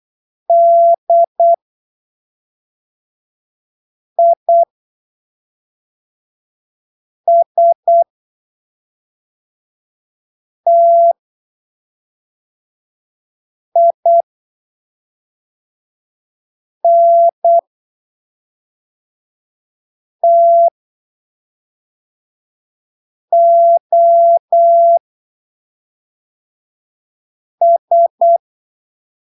Morse code, heard as a keyed tone, about 8 wpm, spaced as for 3 wpm, 680 Hz. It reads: DISTINTOS
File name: arquivos / texto / texto 22 08ppm